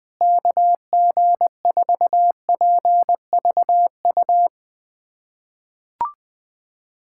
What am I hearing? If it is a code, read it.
Morse code, heard as a keyed tone, about 20 wpm, 700 Hz. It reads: KG4PVU
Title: USA Callsign DE Callsign - 6 Characters - 20wpm